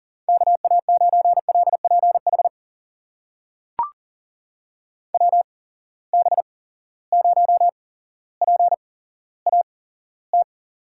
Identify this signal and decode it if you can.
Morse code, heard as a keyed tone, about 40 wpm, spaced as for 17 wpm, 700 Hz. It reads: KA9LPH WB0PAT
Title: USA Call Signs - 6 Characters - 8x Character Spacing - 40wpm